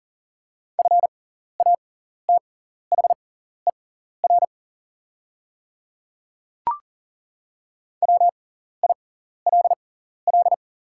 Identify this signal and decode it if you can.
Morse code, heard as a keyed tone, about 40 wpm, spaced as for 14 wpm, 690 Hz. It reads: FATHER WILL
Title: 6x Character Spacing - Top 200 Words - 40wpm